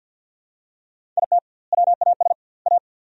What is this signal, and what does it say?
Morse code, heard as a keyed tone, about 52 wpm, 700 Hz. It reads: IT WAS A
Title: 2x Word-Spacing - Sentences from Top 500 Words - 50wpm